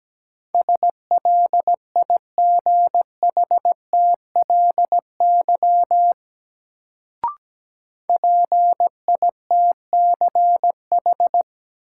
Morse code, heard as a keyed tone, about 17 wpm, 700 Hz. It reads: SLIGHTLY PITCH
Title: Sentences from Top 1300 Words - New Word Drill Alt - 17wpm